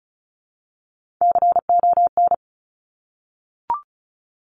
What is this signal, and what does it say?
Morse code, heard as a keyed tone, about 36 wpm, 700 Hz. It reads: CON
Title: Top Three Letter Combinations in Words 35wpm